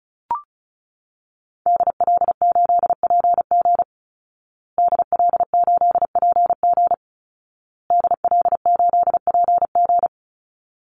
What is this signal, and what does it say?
Morse code, heard as a keyed tone, about 34 wpm, 700 Hz. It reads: DL8PG DL8PG DL8PG
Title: Call Signs Repeated 2x - 35wpm